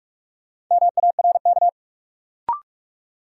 Morse code, heard as a keyed tone, about 44 wpm, 700 Hz. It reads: MARK
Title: Sentences from Top 300 Words - New Word Drill - 45wpm